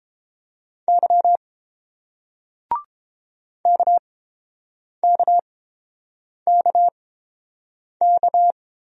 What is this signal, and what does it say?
Morse code, heard as a keyed tone, about 31 wpm, spaced as for 7 wpm, 700 Hz. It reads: Y KKKK